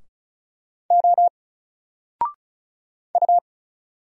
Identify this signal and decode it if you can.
Morse code, heard as a keyed tone, about 34 wpm, 700 Hz. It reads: O U
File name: Single-Letter-Number-35wpm